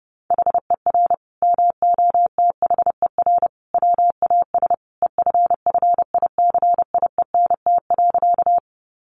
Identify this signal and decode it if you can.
Morse code, heard as a keyed tone, about 30 wpm, 700 Hz. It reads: HER MOTHER WAS EFFICIENT.
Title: Sentences from Top 1300 Words 30wpm